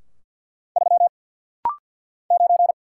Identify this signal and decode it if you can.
Morse code, heard as a keyed tone, about 49 wpm, 700 Hz. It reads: 3 9
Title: Single Letter-Number Alt - Rapid-Fire 50wpm